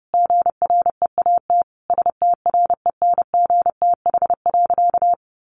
Morse code, heard as a keyed tone, about 30 wpm, 700 Hz. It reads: GREAT STRENGTH.
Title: Sentences from Top 1300 Words 30wpm